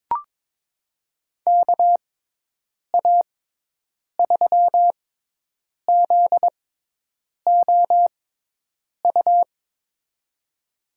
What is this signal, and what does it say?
Morse code, heard as a keyed tone, about 21 wpm, spaced as for 8 wpm, 700 Hz. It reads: KA3ZOU
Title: USA Call Signs - 6 Characters - 6x Character Spacing - 22wpm